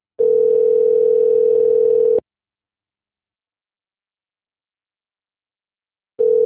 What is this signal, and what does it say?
Ringback tone (North America), 440 + 480 Hz.